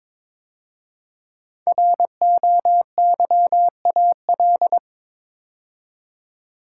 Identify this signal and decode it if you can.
Morse code, heard as a keyed tone, about 22 wpm, 700 Hz. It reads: ROYAL
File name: 5 Letter Words - First 3 Letters Given - 22wpm